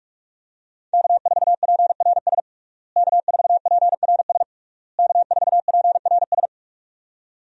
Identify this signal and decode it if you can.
Morse code, heard as a keyed tone, about 46 wpm, 700 Hz. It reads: K4PRS K4PRS K4PRS
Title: USA Call Signs - 5 Characters - Repeated 3x - 45wpm